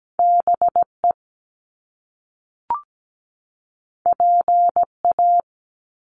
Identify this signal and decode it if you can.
Morse code, heard as a keyed tone, about 17 wpm, 700 Hz. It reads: BE PA
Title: Top Two Letter Combinations in Words 17wpm